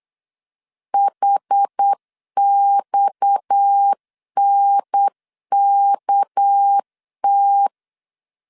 Morse code, heard as a keyed tone, about 8 wpm, 800 Hz. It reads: HXNKT
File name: LEZIONI / 6 PKX0 / 6 PKX 8wpm 3041